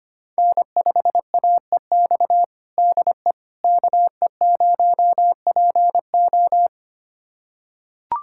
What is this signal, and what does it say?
Morse code, heard as a keyed tone, about 25 wpm, 700 Hz. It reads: N5AEX DE KE0PO
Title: USA Callsign DE Callsign - 5 Characters - 25wpm